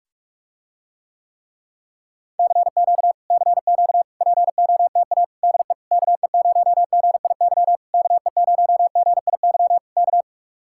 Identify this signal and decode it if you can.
Morse code, heard as a keyed tone, about 49 wpm, 700 Hz. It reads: CQ CQ POTA DE KE0GIY KE0GIY K